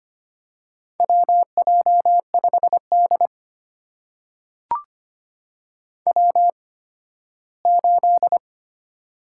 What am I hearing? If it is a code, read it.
Morse code, heard as a keyed tone, about 25 wpm, spaced as for 15 wpm, 700 Hz. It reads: WJ5D W8
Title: USA Call Signs - 4 Characters - 8x Character Spacing - 25wpm